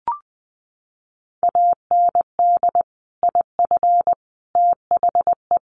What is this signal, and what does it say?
Morse code, heard as a keyed tone, about 20 wpm, 700 Hz. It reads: AND IF THE
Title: Sentences from Top 1300 Words 20wpm